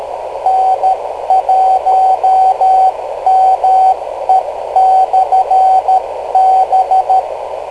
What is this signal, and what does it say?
Morse code, heard as a keyed tone, about 13 wpm, 760 Hz, with radio noise. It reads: N1ME/B
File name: kl7fdq / beacons / n1me-rst539